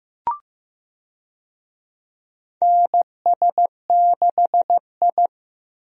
Morse code, heard as a keyed tone, about 15 wpm, 700 Hz. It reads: NS6I
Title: USA Call Signs - 4 Characters - 15wpm